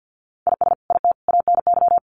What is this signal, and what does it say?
Morse code, heard as a keyed tone, about 50 wpm, 700 Hz. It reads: IS IT ANY
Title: Sentences from Top 200 Words 50wpm